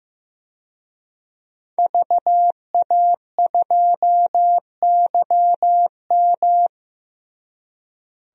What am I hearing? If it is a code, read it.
Morse code, heard as a keyed tone, about 15 wpm, 700 Hz. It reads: VA2YM
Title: DXCC-100 Call Signs - 5 Characters - Repeated 3x - 15wpm